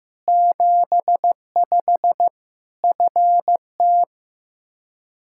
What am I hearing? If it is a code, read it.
Morse code, heard as a keyed tone, about 14 wpm, 700 Hz. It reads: 75 FT
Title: Practice QSO - Exchange 2 - Info - 15wpm